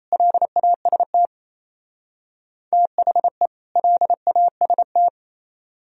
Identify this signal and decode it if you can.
Morse code, heard as a keyed tone, about 36 wpm, 700 Hz. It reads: LAST THE LAST